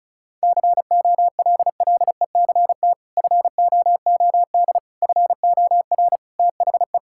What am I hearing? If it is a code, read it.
Morse code, heard as a keyed tone, about 36 wpm, 700 Hz. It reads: COLLECT FOOD FOR THE
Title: Sentences from Top 1600 Words 35wpm